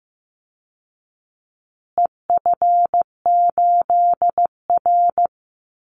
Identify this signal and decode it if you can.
Morse code, heard as a keyed tone, about 15 wpm, 700 Hz. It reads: EF8R
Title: Call Signs Repeated 2x - 15wpm